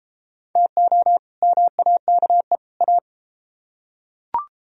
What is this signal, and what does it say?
Morse code, heard as a keyed tone, about 29 wpm, 700 Hz. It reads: TO MAKE A